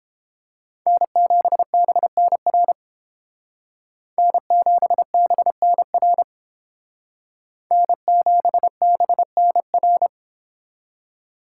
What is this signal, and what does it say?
Morse code, heard as a keyed tone, about 30 wpm, 700 Hz. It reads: N7BNR N7BNR N7BNR